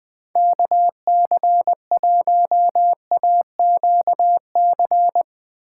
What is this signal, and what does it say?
Morse code, heard as a keyed tone, about 22 wpm, 700 Hz. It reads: KC1AQC
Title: Speed Racing - USA Call Signs 20wpm